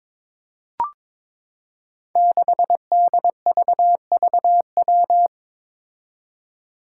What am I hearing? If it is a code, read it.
Morse code, heard as a keyed tone, about 22 wpm, 700 Hz. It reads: 6DVVW
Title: Five Challenging Letter-Number Combinations 22wpm